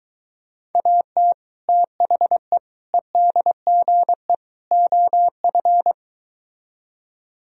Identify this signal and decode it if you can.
Morse code, heard as a keyed tone, about 22 wpm, 700 Hz. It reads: AT THE EDGE OF